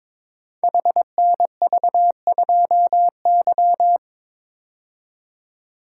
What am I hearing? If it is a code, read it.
Morse code, heard as a keyed tone, about 22 wpm, 700 Hz. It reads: HNV2Y
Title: Code Groups - 5 Letter:Numbers - 22wpm